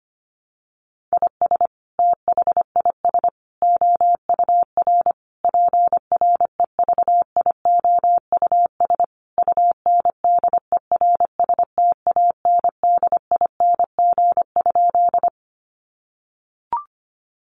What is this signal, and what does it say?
Morse code, heard as a keyed tone, about 25 wpm, 700 Hz. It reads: IS THIS OUR PREVIOUS UNDERSTANDING?
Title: Sentences from Top 1500 Words 25wpm